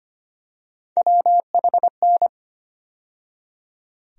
Morse code, heard as a keyed tone, about 25 wpm, 700 Hz. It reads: WHN
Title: USA Call Sign Postfixes - 3 Letters 25wpm